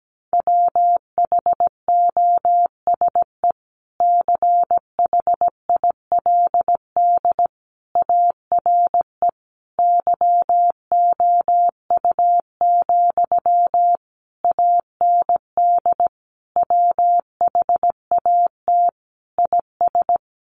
Morse code, heard as a keyed tone, about 17 wpm, 700 Hz. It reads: WHOSE CHILD ARE YOU, AND WHAT IS
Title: Sentences from Top 1200 Words 17wpm